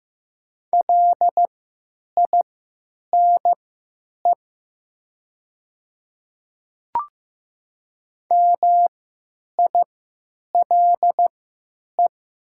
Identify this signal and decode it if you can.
Morse code, heard as a keyed tone, about 15 wpm, spaced as for 9 wpm, 700 Hz. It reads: LINE MILE
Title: 3x Character Spacing - Top 300 Words - 15wpm